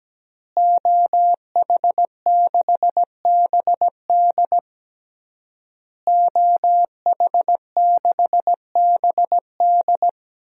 Morse code, heard as a keyed tone, about 17 wpm, 700 Hz. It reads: OH6BD OH6BD
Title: DXCC-100 Call Signs - 5 Characters - Repeated 3x - 17wpm